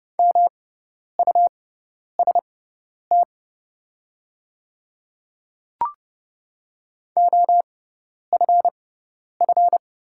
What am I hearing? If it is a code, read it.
Morse code, heard as a keyed tone, about 30 wpm, spaced as for 10 wpm, 700 Hz. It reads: MUST OFF
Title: 6x Character Spacing - Top 200 Words - 30wpm